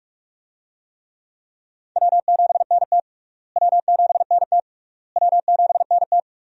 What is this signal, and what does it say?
Morse code, heard as a keyed tone, about 46 wpm, 700 Hz. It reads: W7NT W7NT W7NT
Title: USA Call Signs - 4 Characters - Repeated 3x - 45wpm